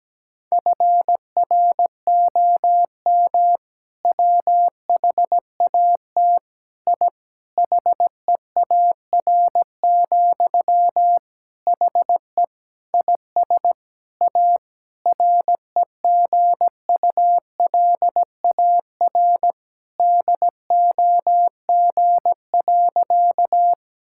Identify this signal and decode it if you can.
Morse code, heard as a keyed tone, about 17 wpm, 700 Hz. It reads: FROM WHAT I HEAR, HE IS A REGULAR DOG.
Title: Sentences from Top 1000 Words - Review Part 6 - 17wpm